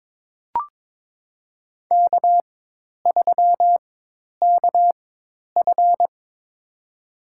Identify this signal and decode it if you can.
Morse code, heard as a keyed tone, about 22 wpm, spaced as for 10 wpm, 700 Hz. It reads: K3KF
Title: USA Call Signs - 4 Characters - 4x Character Spacing - 22wpm